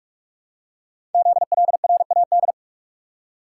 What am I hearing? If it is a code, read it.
Morse code, heard as a keyed tone, about 44 wpm, 700 Hz. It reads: ZLRAD
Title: Code Groups - 5 Letter:Numbers - 45wpm